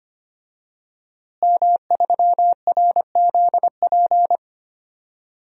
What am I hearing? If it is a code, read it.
Morse code, heard as a keyed tone, about 25 wpm, 700 Hz. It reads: M3RZP
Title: DXCC-100 Call Signs - 5 Characters - Repeated 2x - 25wpm